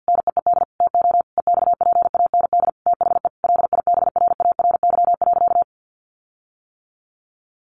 Morse code, heard as a keyed tone, about 48 wpm, 700 Hz. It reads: NEED TO EXPAND THE LIBRARY.
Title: Sentences from Top 1400 Words 50wpm